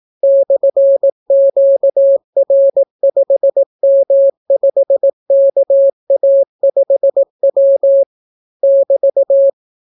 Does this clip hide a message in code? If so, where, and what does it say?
Yes: Morse code from 0.2 to 9.5 s, heard as a keyed tone, about 18 wpm, 550 Hz. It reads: /QR5M5KA5W <BT>